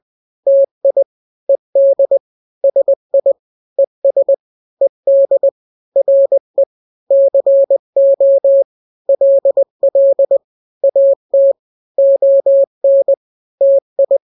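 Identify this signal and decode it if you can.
Morse code, heard as a keyed tone, about 20 wpm, 550 Hz. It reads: TI ED SI ES ED RE CO LL AT ON TI
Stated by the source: Morse drill of two-letter n-grams at 20/18 WPM (Farnsworth), from KK7RR.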